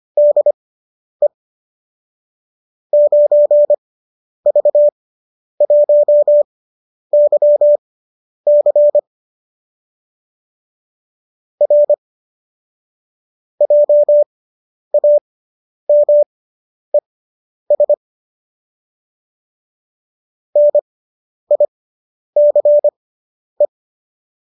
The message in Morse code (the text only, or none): DE 9V1YC R JAMES NICE